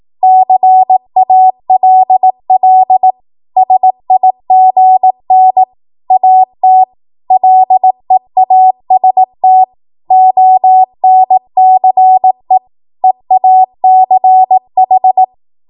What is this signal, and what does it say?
Morse code, heard as a keyed tone, about 18 wpm, 750 Hz. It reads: CALL SIGN AT LEAST ONCE EACH
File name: Morse / 18WPM